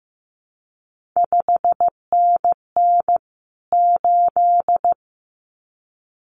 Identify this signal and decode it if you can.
Morse code, heard as a keyed tone, about 15 wpm, 700 Hz. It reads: 5NN 8
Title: CQ WW DX Contest Exchange - 15wpm